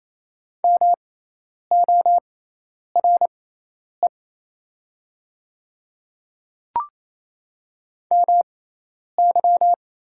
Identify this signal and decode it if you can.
Morse code, heard as a keyed tone, about 28 wpm, spaced as for 10 wpm, 700 Hz. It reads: MORE MY
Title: 6x Character Spacing - Top 100 Words - 28wpm